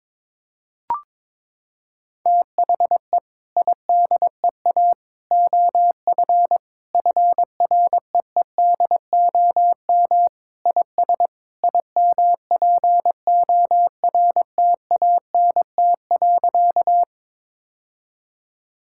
Morse code, heard as a keyed tone, about 22 wpm, 700 Hz. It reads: THE IDEA OF FREEDOM IS IMPORTANT.
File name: practice / 22wpm / Sentences with 3-Word Phrases - 22wpm